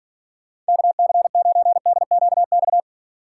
Morse code, heard as a keyed tone, about 52 wpm, 700 Hz. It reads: KC9DQX